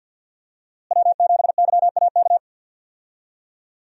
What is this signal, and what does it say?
Morse code, heard as a keyed tone, about 50 wpm, 700 Hz. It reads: W7YAK
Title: USA Call Signs - 5 Characters - Repeated 3x - 50wpm